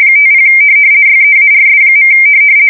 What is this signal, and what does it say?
RTTY, about 50 baud. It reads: 50 YEARS.
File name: fsk170-50B / teletype-corp